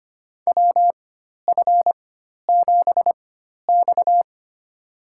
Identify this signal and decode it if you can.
Morse code, heard as a keyed tone, about 25 wpm, spaced as for 12 wpm, 700 Hz. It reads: WF7X
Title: USA Call Signs - 4 Characters - 4x Character Spacing - 25wpm